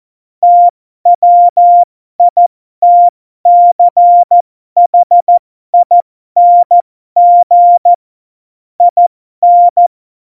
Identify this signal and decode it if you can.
Morse code, heard as a keyed tone, about 14 wpm, spaced as for 12 wpm, 700 Hz. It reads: TWITCHING IN
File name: Morsekurs / Witze